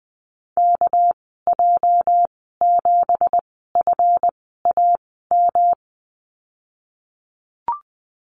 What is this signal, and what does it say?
Morse code, heard as a keyed tone, about 19 wpm, spaced as for 14 wpm, 700 Hz. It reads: KJ7FAM
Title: 219 Lesson 35g - Number 0 - Call Signs 20wpm